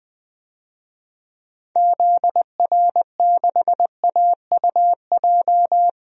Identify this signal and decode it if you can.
Morse code, heard as a keyed tone, about 20 wpm, 700 Hz. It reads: ZR6AUJ
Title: DXCC-100 Call Signs - 6 Characters - 20wpm